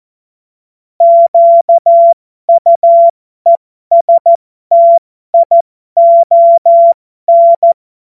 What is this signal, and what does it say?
Morse code, heard as a keyed tone, about 14 wpm, spaced as for 12 wpm, 680 Hz. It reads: QUESTION
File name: Morsekurs / Witze